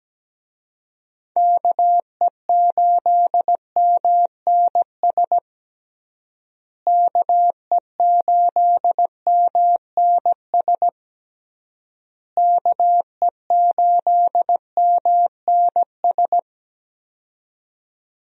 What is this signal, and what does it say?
Morse code, heard as a keyed tone, about 17 wpm, 700 Hz. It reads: KE8MNS KE8MNS KE8MNS
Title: USA Call Signs - 6 Characters - Repeated 3x - 17wpm